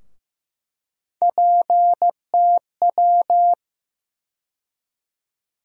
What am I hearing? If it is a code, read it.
Morse code, heard as a keyed tone, about 15 wpm, 700 Hz. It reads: PTW
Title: USA Call Sign Postfixes - 3 Letters 15wpm